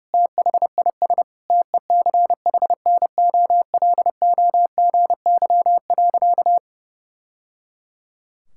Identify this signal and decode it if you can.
Morse code, heard as a keyed tone, about 28 wpm, 700 Hz. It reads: THIS TECHNOLOGY.
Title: Sentences from Top 1000 Words - Review Part 5 - 30wpm